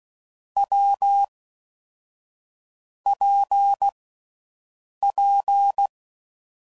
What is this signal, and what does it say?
Morse code, heard as a keyed tone, about 16 wpm, 800 Hz. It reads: W P P
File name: Combinations / Difficult Combos / WPJ WPJ1